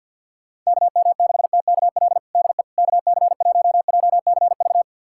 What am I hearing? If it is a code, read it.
Morse code, heard as a keyed tone, about 51 wpm, 700 Hz. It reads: KM6TKL DE KC1JCV
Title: USA Callsign DE Callsign 50wpm